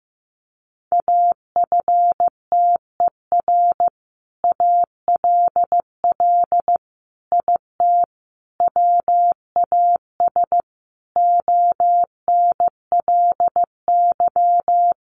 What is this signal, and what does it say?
Morse code, heard as a keyed tone, about 14 wpm, 700 Hz. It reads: AFTER ALL IT WAS ONLY
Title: Sentences from Top 1200 Words 15wpm